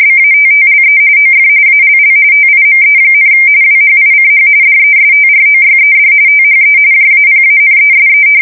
RTTY, about 45.45 baud. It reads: BATTLE OF ALL TIME, WHICH DESTROYED THE JAPANESE
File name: fsk170 / battleshipnj